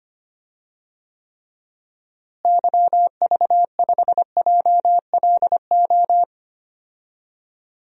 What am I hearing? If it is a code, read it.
Morse code, heard as a keyed tone, about 25 wpm, 700 Hz. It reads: YV5JLO
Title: DXCC-100 Call Signs - 6 Characters - Repeated 3x - 25wpm